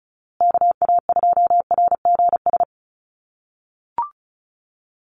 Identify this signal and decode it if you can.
Morse code, heard as a keyed tone, about 34 wpm, 700 Hz. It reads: KA2RGS
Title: USA Call Signs Repeated 3x 35wpm